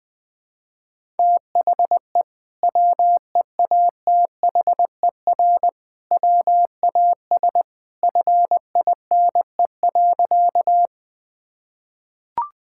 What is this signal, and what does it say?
Morse code, heard as a keyed tone, about 21 wpm, 700 Hz. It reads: THE WEATHER WAS FINE.
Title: Sentences from Top 1000 Words - Review Part 5 - 20wpm